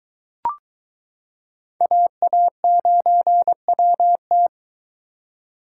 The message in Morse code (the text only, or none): AA9WT